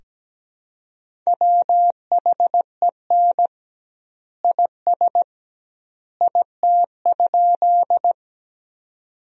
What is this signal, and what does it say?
Morse code, heard as a keyed tone, about 17 wpm, 700 Hz. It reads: WHEN IS IT?
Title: 2x Word-Spacing - Sentences from Top 100 Words - 17wpm